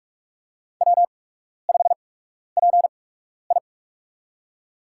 Morse code, heard as a keyed tone, about 42 wpm, spaced as for 12 wpm, 700 Hz. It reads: W5PI
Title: USA Call Signs - 4 Characters - 8x Character Spacing - 45wpm